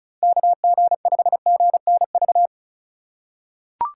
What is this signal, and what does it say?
Morse code, heard as a keyed tone, about 36 wpm, 700 Hz. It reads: KG5GNV
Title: USA Call Signs - 6 Characters - 2x Character Spacing - 35wpm